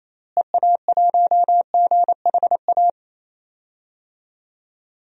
Morse code, heard as a keyed tone, about 28 wpm, 700 Hz. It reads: EA1GHA
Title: DXCC-100 Call Signs - 6 Characters - Repeated 3x - 28wpm